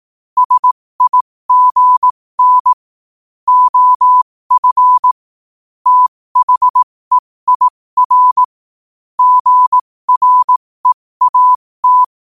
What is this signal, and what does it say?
Morse code, heard as a keyed tone, about 18 wpm, spaced as for 15 wpm, 1000 Hz. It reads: SIGN OF THEIR GREAT
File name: 15wpm